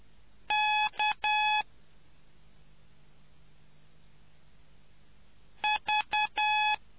Morse code, heard as a keyed tone, about 10 wpm, 850 Hz. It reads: K V